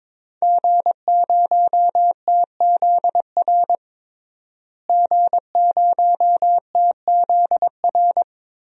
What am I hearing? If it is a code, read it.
Morse code, heard as a keyed tone, about 22 wpm, 700 Hz. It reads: G0TZR G0TZR
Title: DXCC-100 Call Signs - 5 Characters - Repeated 2x - 22wpm